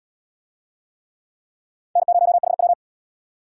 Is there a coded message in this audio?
Yes: Morse code from 2.0 to 2.7 s, heard as a keyed tone, about 77 wpm, 700 Hz. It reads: N0HW